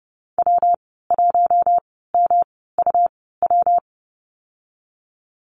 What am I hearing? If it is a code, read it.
Morse code, heard as a keyed tone, about 30 wpm, spaced as for 17 wpm, 700 Hz. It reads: W1MUW
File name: Lesson 19g - Letter w - Call Signs 30wpm